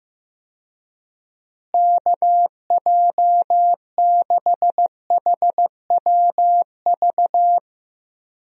Morse code, heard as a keyed tone, about 15 wpm, 700 Hz. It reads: KJ6HWV
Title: USA Call Signs - 6 Characters - Repeated 2x - 15wpm